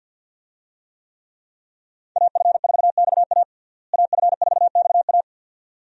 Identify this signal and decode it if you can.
Morse code, heard as a keyed tone, about 52 wpm, 700 Hz. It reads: AF4XA AF4XA
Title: USA Call Signs - 5 Characters - Repeated 2x - 50wpm